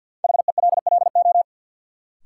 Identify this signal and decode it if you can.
Morse code, heard as a keyed tone, about 48 wpm, 700 Hz. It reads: HELLO
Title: Sentences from Top 1100 Words - New Word Drill - 50wpm